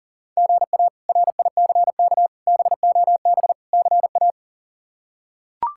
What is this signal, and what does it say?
Morse code, heard as a keyed tone, about 40 wpm, 700 Hz. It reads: GA RICK BOB CA
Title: K1USN SST QSO Practice - 40wpm